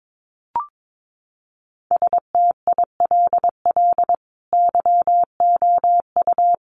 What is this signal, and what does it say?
Morse code, heard as a keyed tone, about 22 wpm, 700 Hz. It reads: STILL YOU
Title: Sentences from Top 600 Words 22wpm